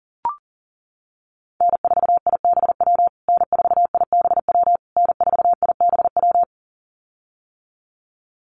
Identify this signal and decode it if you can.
Morse code, heard as a keyed tone, about 37 wpm, 700 Hz. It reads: N4IBW N4IBW N4IBW
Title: USA Call Signs Repeated 3x 40wpm